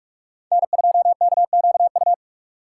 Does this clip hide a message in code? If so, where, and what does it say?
Yes: Morse code from 0.5 to 2.1 s, heard as a keyed tone, about 48 wpm, 700 Hz. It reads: N2KQU